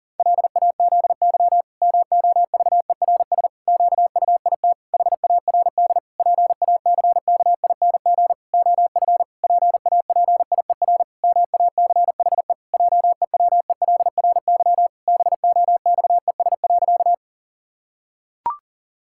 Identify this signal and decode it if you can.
Morse code, heard as a keyed tone, about 40 wpm, 700 Hz. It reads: LAZY MOVERS QUIT HARD PACKING OF PAPIER MACHE JEWELRY BOXES.
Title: Pangram Challenge v2 - 40wpm